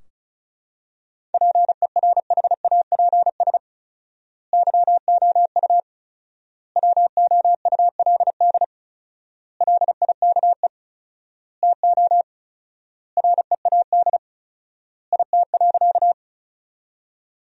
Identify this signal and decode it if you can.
Morse code, heard as a keyed tone, about 34 wpm, 700 Hz. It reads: PERHAPS YOU WOULD LIKE TO READ IT.
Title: 4x Word-Spacing - Sentences from Top 500 Words - 35wpm